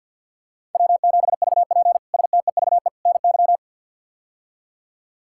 Morse code, heard as a keyed tone, about 48 wpm, 690 Hz. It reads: W7VP STEVE NY
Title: K1USN SST QSO Practice - 50wpm